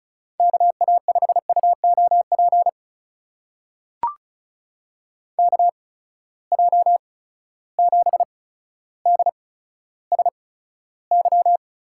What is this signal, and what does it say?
Morse code, heard as a keyed tone, about 36 wpm, spaced as for 15 wpm, 700 Hz. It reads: KA5UOP KJ7DSY
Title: USA Call Signs - 6 Characters - 8x Character Spacing - 35wpm